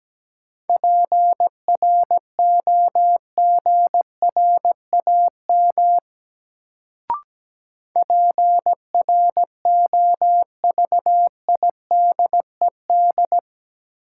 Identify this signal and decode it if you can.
Morse code, heard as a keyed tone, about 17 wpm, 700 Hz. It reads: PROGRAM PROVIDED
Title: Sentences from Top 1500 Words - New Word Drill Alt - 17wpm